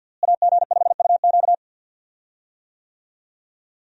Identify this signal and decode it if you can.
Morse code, heard as a keyed tone, about 52 wpm, 700 Hz. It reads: AG5UQ
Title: USA Call Signs - 5 Characters - Repeated 3x - 50wpm